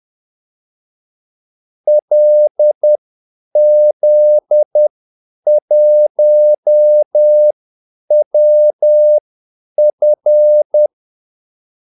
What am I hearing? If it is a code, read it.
Morse code, heard as a keyed tone, about 10 wpm, spaced as for 8 wpm, 600 Hz. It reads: LZ1WF
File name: Callsigns-10-8wpm